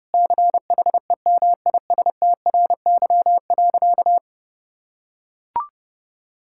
Morse code, heard as a keyed tone, about 32 wpm, 700 Hz. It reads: CHEMISTRY.
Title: Sentences from Top 1000 Words - Review Part 6 - 30wpm